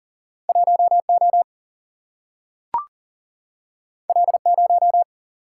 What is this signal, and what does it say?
Morse code, heard as a keyed tone, about 40 wpm, 700 Hz. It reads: 1 O L 0
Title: Single-Letter Single-Number 40wpm